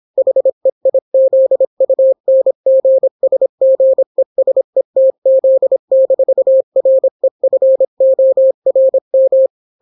Morse code, heard as a keyed tone, about 26 wpm, 530 Hz. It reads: HEIZUNGSGESETZ-REFORM